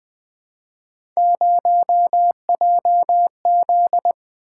Morse code, heard as a keyed tone, about 20 wpm, 700 Hz. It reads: 0JZ
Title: Three Letter-Number Combinations - 20wpm